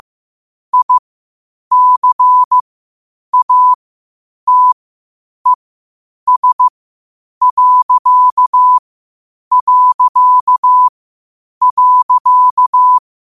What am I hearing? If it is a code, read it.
Morse code, heard as a keyed tone, about 15 wpm, spaced as for 8 wpm, 1000 Hz. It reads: ICATES...